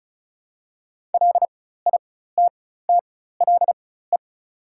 Morse code, heard as a keyed tone, about 34 wpm, spaced as for 16 wpm, 690 Hz. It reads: LITTLE
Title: 4x Character Spacing - Top 300 Words - 35wpm